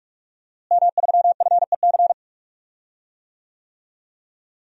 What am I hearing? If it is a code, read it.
Morse code, heard as a keyed tone, about 45 wpm, 700 Hz. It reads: M3FEC
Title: DXCC-100 Call Signs - 5 Characters - Repeated 2x - 45wpm